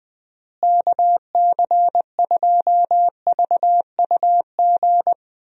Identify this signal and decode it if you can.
Morse code, heard as a keyed tone, about 20 wpm, 700 Hz. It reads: KC2VUG
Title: USA Call Signs - 6 Characters - Repeated 3x - 20wpm